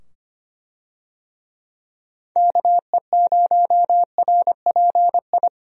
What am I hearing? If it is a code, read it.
Morse code, heard as a keyed tone, about 25 wpm, 700 Hz. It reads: KE0RPI
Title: USA Call Signs - 6 Characters - 25wpm